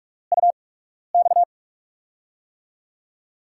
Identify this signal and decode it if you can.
Morse code, heard as a keyed tone, about 46 wpm, spaced as for 12 wpm, 700 Hz. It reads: UX
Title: USA Call Signs - 5 Characters - 8x Character Spacing - 45wpm